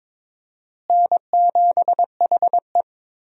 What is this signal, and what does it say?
Morse code, heard as a keyed tone, about 22 wpm, 700 Hz. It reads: N7HE
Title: USA Call Signs - 4 Characters - Repeated 3x - 22wpm